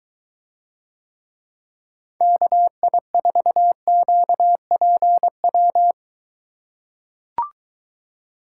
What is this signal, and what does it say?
Morse code, heard as a keyed tone, about 23 wpm, 700 Hz. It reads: KI4QPW